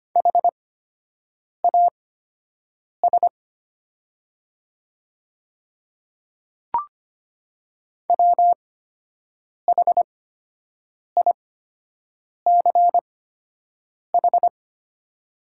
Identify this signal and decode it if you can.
Morse code, heard as a keyed tone, about 25 wpm, spaced as for 7 wpm, 690 Hz. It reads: HAS WHICH
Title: 8x Character Spacing - Top 100 Words - 25wpm